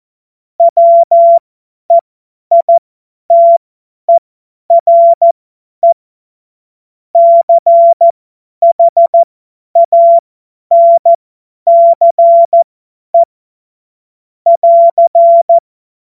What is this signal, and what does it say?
Morse code, heard as a keyed tone, about 14 wpm, spaced as for 10 wpm, 680 Hz. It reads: WEITERE CHANCE <AR>
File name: Morsekurs / Witze / Witz 06. Dezember 2022 50 F70